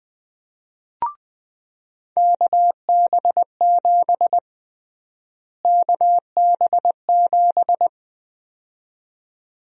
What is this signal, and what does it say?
Morse code, heard as a keyed tone, about 20 wpm, 700 Hz. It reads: KB7 KB7
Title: USA Call Sign Prefixes - Repeated 2x - 20wpm